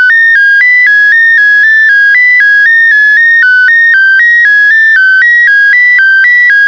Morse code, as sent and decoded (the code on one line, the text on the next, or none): none